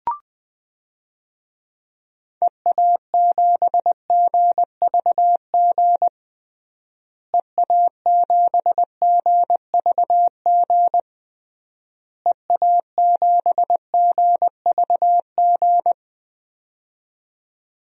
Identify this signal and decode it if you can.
Morse code, heard as a keyed tone, about 20 wpm, 700 Hz. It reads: EA7GVG EA7GVG EA7GVG
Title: DXCC-100 Call Signs - 6 Characters - Repeated 3x - 20wpm